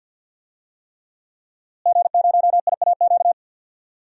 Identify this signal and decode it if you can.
Morse code, heard as a keyed tone, about 52 wpm, 700 Hz. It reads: G0IAQ
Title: DXCC-100 Call Signs - 5 Characters - Repeated 3x - 50wpm